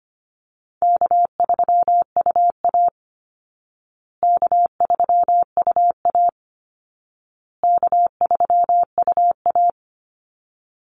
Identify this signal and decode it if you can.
Morse code, heard as a keyed tone, about 25 wpm, 700 Hz. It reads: K3UA K3UA K3UA
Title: Easy Call Signs Repeated 3x - 25wpm